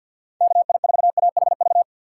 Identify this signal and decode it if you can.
Morse code, heard as a keyed tone, about 52 wpm, 700 Hz. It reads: KI4AHV
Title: USA Call Signs - 2x Character Spacing - 50wpm